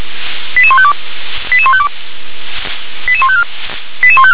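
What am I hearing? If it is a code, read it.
Five-tone selective call (ZVEI), heard as quick runs of five tones, tones of about 70 ms. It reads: 88152 88152 88154 88154
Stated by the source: ZVEI five-tone call received on an SDR.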